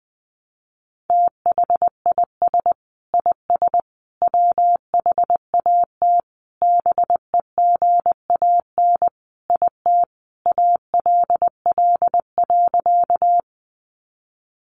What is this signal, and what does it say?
Morse code, heard as a keyed tone, about 20 wpm, 700 Hz. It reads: THIS IS WHAT BEGAN IT ALL.
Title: Sentences from Top 300 Words 20wpm